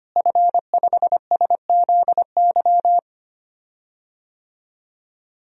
Morse code, heard as a keyed tone, about 25 wpm, 700 Hz. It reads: F5SZY
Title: DXCC-100 Call Signs - 5 Characters - Repeated 3x - 25wpm